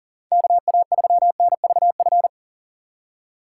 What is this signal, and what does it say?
Morse code, heard as a keyed tone, about 40 wpm, 700 Hz. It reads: KA3NVF
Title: USA Call Signs - 6 Characters - 4x Character Spacing - 40wpm